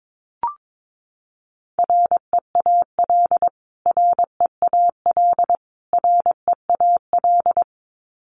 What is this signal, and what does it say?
Morse code, heard as a keyed tone, about 22 wpm, 700 Hz. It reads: REAL REAL REAL
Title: Top 300 Words Repeated 22wpm